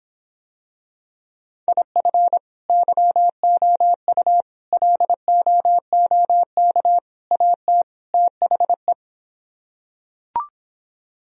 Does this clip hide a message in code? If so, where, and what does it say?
Yes: Morse code from 1.7 to 8.9 s, heard as a keyed tone, about 26 wpm, 700 Hz. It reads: IF YOU LOOK AT THE